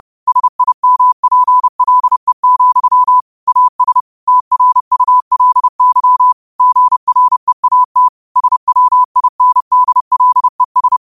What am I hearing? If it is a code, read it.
Morse code, heard as a keyed tone, about 30 wpm, 1000 Hz. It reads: SIMPLE, AS TRULY GREAT SWINDLES
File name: qotdfiles / 30wpm